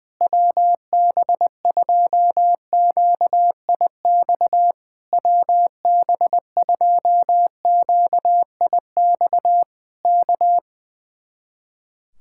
Morse code, heard as a keyed tone, about 20 wpm, 700 Hz. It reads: WB2QIX WB2QIX K
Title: Speed-Racing Common QSO Phrases - 20wpm